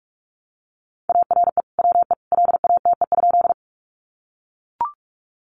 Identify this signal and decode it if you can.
Morse code, heard as a keyed tone, about 44 wpm, 700 Hz. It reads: ARE WE LATE?
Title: Sentences from Top 300 Words 45wpm